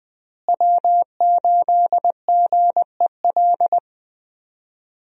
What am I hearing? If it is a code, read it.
Morse code, heard as a keyed tone, about 20 wpm, 700 Hz. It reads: W8GEL
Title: USA Call Signs - 5 Characters - Repeated 2x - 20wpm